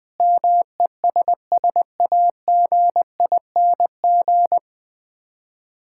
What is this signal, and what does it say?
Morse code, heard as a keyed tone, about 20 wpm, 700 Hz. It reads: MESSAGING
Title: Easy Words - Speed-Racing with Hints - 20wpm